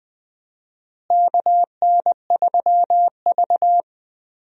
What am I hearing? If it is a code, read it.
Morse code, heard as a keyed tone, about 21 wpm, 700 Hz. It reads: KN3V
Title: USA Call Signs - 4 Characters - Repeated 3x - 20wpm